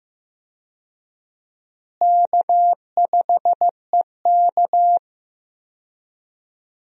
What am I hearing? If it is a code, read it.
Morse code, heard as a keyed tone, about 15 wpm, 700 Hz. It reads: K5EK
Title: USA Call Signs - 4 Characters - Repeated 3x - 15wpm